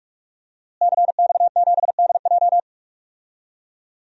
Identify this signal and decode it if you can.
Morse code, heard as a keyed tone, about 42 wpm, 700 Hz. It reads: CX7DJ
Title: DXCC-100 Call Signs - 5 Characters - Repeated 3x - 45wpm